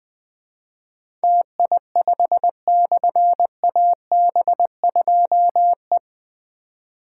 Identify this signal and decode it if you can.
Morse code, heard as a keyed tone, about 20 wpm, 700 Hz. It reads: TI5/AB2E
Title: Difficult Call Signs 20wpm